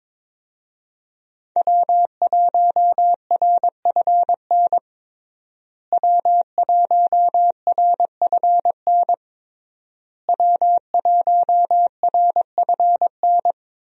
Morse code, heard as a keyed tone, about 22 wpm, 700 Hz. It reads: W1RFN W1RFN W1RFN
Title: USA Call Signs - 5 Characters - Repeated 3x - 22wpm